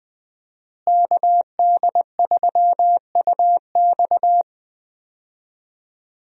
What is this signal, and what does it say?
Morse code, heard as a keyed tone, about 20 wpm, 700 Hz. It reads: KD3UX
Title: USA Call Signs - 5 Characters - Repeated 3x - 20wpm